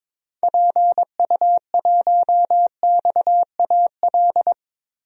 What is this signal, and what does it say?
Morse code, heard as a keyed tone, about 21 wpm, 700 Hz. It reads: PU1XAL
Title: DXCC-100 Call Signs - 6 Characters - Repeated 3x - 22wpm